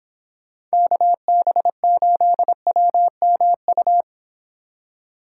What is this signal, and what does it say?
Morse code, heard as a keyed tone, about 26 wpm, 700 Hz. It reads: KB8WMU